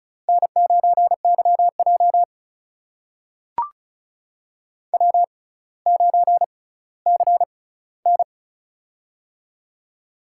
Morse code, heard as a keyed tone, about 36 wpm, spaced as for 18 wpm, 700 Hz. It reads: N9YJ W9CN
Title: USA Call Signs - 6x Character Spacing - 35wpm